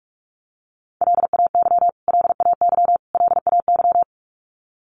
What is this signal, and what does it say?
Morse code, heard as a keyed tone, about 44 wpm, 700 Hz. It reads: LAY LAY LAY
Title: Top 500 Words Repeated 45wpm